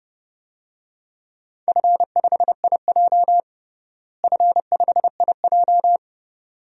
Morse code, heard as a keyed tone, about 30 wpm, 700 Hz. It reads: F5IJ F5IJ
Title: DXCC-100 Call Signs - 4 Characters - Repeated 2x - 30wpm